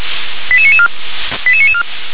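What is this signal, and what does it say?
Five-tone selective call (ZVEI), heard as quick runs of five tones, tones of about 70 ms. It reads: 88994 88994
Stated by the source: ZVEI five-tone call 88774 received via SDR.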